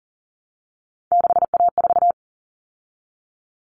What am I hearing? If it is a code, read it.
Morse code, heard as a keyed tone, about 40 wpm, 700 Hz. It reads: 6A4
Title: Common Cut Numbers - 3-Digit 40wpm